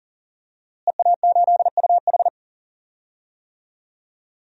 Morse code, heard as a keyed tone, about 40 wpm, 700 Hz. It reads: EA8UH
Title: DXCC-100 Call Signs - 5 Characters - Repeated 2x - 40wpm